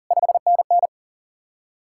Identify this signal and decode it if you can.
Morse code, heard as a keyed tone, about 48 wpm, 700 Hz. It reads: 5NN
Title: Speed-Racing Common QSO Phrases Encore - 40wpm